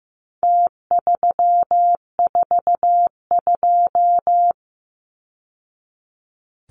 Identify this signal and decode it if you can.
Morse code, heard as a keyed tone, about 15 wpm, 700 Hz. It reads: T342
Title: Numbers - Contest Style - 4-Digit 15wpm